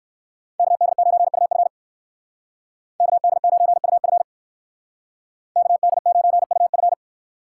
Morse code, heard as a keyed tone, about 58 wpm, 700 Hz. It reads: KD9UF KD9UF KD9UF